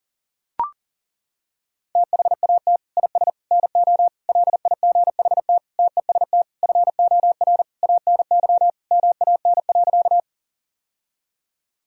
Morse code, heard as a keyed tone, about 40 wpm, 700 Hz. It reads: THAT IS NO LIGHT TEST FOR ANY MAN.
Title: Sentences from Top 1000 Words - Review Part 3 - 40wpm